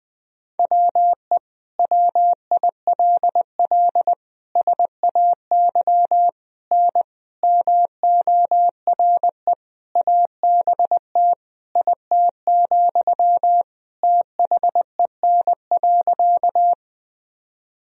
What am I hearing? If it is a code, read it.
Morse code, heard as a keyed tone, about 21 wpm, 700 Hz. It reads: WE WILL SAY N MORE ABT IT, THEN.
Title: Sentences from Top 200 Words and Abbreviations 20wpm